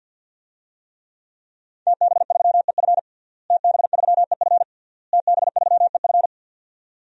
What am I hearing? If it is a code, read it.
Morse code, heard as a keyed tone, about 50 wpm, 700 Hz. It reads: TB3EF TB3EF TB3EF
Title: DXCC-100 Call Signs - 5 Characters - Repeated 3x - 50wpm